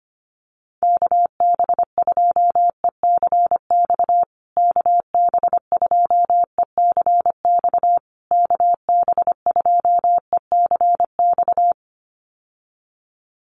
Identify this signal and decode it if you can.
Morse code, heard as a keyed tone, about 25 wpm, 700 Hz. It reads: KB2ECX KB2ECX KB2ECX
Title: USA Call Signs Repeated 3x 25wpm